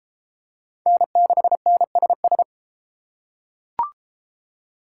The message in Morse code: N6NSS

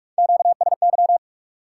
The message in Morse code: QSY

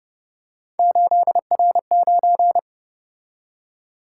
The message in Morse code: 8R9

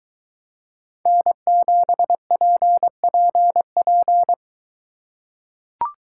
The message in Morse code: N7PPP

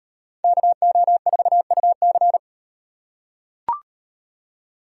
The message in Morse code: KO4UC

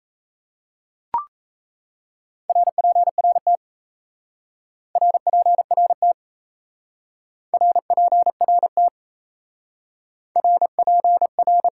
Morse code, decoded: RPRT RPRT RPRT RPR

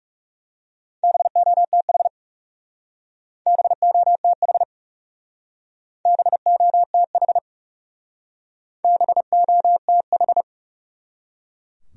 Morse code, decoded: BOTH BOTH BOTH BOTH

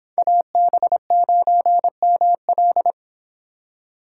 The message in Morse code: AB9ML

